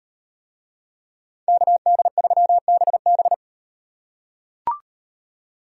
Morse code, heard as a keyed tone, about 35 wpm, 700 Hz. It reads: KD3BB